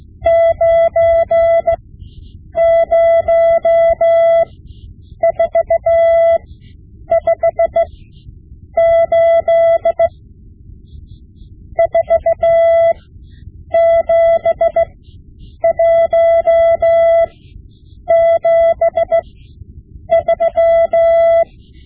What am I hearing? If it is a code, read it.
Morse code, heard as a keyed tone, about 16 wpm, spaced as for 8 wpm, 670 Hz. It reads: 90458 47173
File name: Morse Lessons / Morse Lessons / Tape-5B